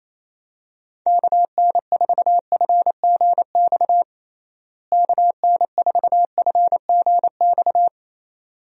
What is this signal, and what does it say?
Morse code, heard as a keyed tone, about 28 wpm, 700 Hz. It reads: KN4FGX KN4FGX
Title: USA Call Signs - 6 Characters - Repeated 2x - 28wpm